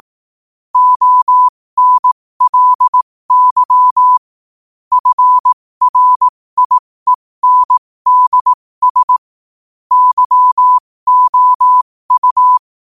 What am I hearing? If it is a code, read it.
Morse code, heard as a keyed tone, about 18 wpm, spaced as for 15 wpm, 1000 Hz. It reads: ONLY FRIENDS YOU
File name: qotdfiles / 15wpm